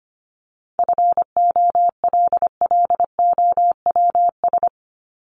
Morse code, heard as a keyed tone, about 25 wpm, 700 Hz. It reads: FOLLOWS
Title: ES:S Suffix - Top 100 Words with Base Words 25wpm